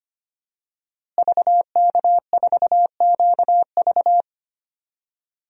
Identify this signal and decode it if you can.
Morse code, heard as a keyed tone, about 25 wpm, 700 Hz. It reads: VK4QV
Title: DXCC-100 Call Signs - 5 Characters - Repeated 3x - 25wpm